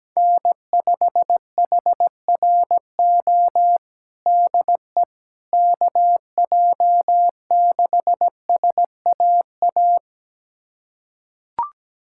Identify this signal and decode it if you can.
Morse code, heard as a keyed tone, about 17 wpm, 700 Hz. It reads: N5HRO DE KJ6SAA
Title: USA Callsign DE Callsign 17wpm